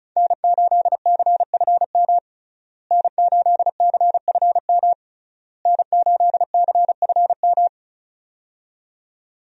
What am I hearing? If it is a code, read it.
Morse code, heard as a keyed tone, about 36 wpm, 700 Hz. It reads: N8CFM N8CFM N8CFM
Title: USA Call Signs - 5 Characters - Repeated 3x - 35wpm